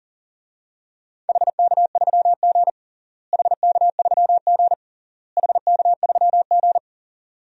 Morse code, heard as a keyed tone, about 40 wpm, 700 Hz. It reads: HK3G HK3G HK3G
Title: DXCC-100 Call Signs - 4 Characters - Repeated 3x - 40wpm